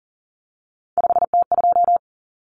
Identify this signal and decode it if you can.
Morse code, heard as a keyed tone, about 40 wpm, 700 Hz. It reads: 5T2
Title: Common Cut Numbers - 3-Digit 40wpm